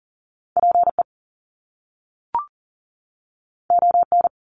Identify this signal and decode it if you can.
Morse code, heard as a keyed tone, about 41 wpm, 700 Hz. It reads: PE ON
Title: Canadian Provinces and Territories 40wpm